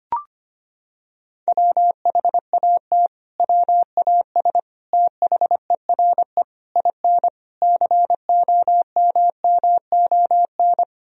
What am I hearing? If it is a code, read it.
Morse code, heard as a keyed tone, about 25 wpm, 700 Hz. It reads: WHAT WAS THERE IN COMMON
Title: Sentences from Top 1000 Words - Review Part 3 - 25wpm